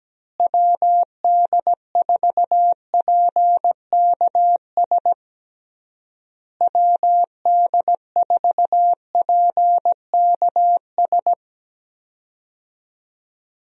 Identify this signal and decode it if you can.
Morse code, heard as a keyed tone, about 17 wpm, 700 Hz. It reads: WD4PKS WD4PKS
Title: USA Call Signs - 6 Characters - Repeated 3x - 17wpm